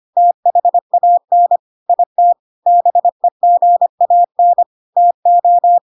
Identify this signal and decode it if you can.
Morse code, heard as a keyed tone, about 25 wpm, 700 Hz. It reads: THAN IT BEGAN TO
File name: Pinocchio / Pinocchio ch3 25 25 ik1hge